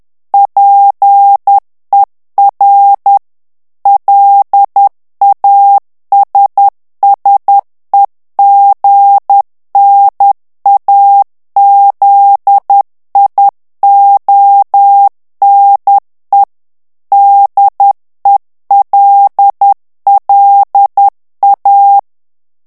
Morse code, heard as a keyed tone, about 10 wpm, 800 Hz. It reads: PER LASSEGNAZIONE DELLA
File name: ITALIEN texte clair 2 vitesse 10 mots minute